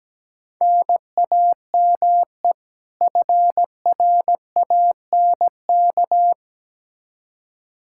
Morse code, heard as a keyed tone, about 17 wpm, 700 Hz. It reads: NAME FRANK
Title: Practice QSO - Exchange 1 - Beginning - 17wpm